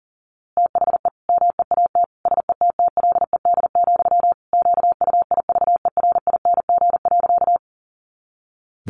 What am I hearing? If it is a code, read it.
Morse code, heard as a keyed tone, about 40 wpm, 700 Hz. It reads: THE MEAT SETTLED, QUIVERING.
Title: A Sound Of Thunder 40wpm